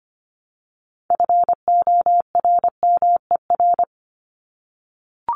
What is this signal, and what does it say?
Morse code, heard as a keyed tone, about 25 wpm, 700 Hz. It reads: FORMER
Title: LY Suffix - Top 100 Words with Base Words 25wpm